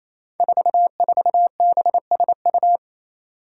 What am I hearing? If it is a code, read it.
Morse code, heard as a keyed tone, about 28 wpm, 700 Hz. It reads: 44BSU
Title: Five Challenging Letter-Number Combinations 28wpm